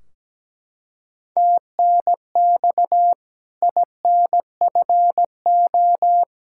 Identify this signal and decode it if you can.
Morse code, heard as a keyed tone, about 17 wpm, 700 Hz. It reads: TNX INFO
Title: K1USN SST QSO Practice 17wpm